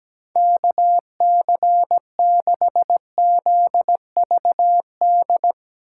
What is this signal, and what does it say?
Morse code, heard as a keyed tone, about 17 wpm, 700 Hz. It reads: KC6ZVD
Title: USA Call Signs - 6 Characters - 3x Character Spacing - 17wpm